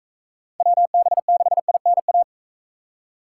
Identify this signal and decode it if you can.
Morse code, heard as a keyed tone, about 41 wpm, 700 Hz. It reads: WB6INA